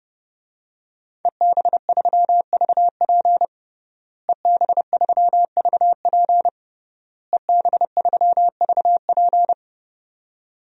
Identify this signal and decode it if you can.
Morse code, heard as a keyed tone, about 30 wpm, 700 Hz. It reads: EB3VP EB3VP EB3VP
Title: DXCC-100 Call Signs - 5 Characters - Repeated 3x - 30wpm